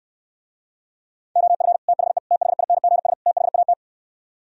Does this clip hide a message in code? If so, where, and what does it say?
Yes: Morse code from 1.4 to 3.7 s, heard as a keyed tone, about 68 wpm, 700 Hz. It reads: OF THE THINGS THAT